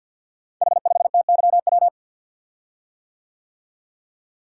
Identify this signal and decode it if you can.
Morse code, heard as a keyed tone, about 52 wpm, 700 Hz. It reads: H5TYW
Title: Code Groups - 5 Letter:Numbers - 50wpm